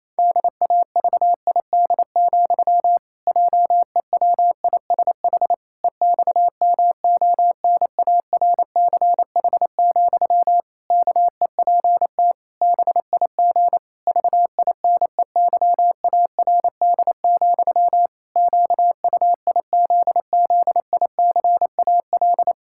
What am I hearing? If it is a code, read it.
Morse code, heard as a keyed tone, about 28 wpm, 700 Hz. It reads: DAVID, JEWISH EXMONARCH, KEPT BIG VINEYARD, QUIZZICAL
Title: Pangram Challenge v2 - 28wpm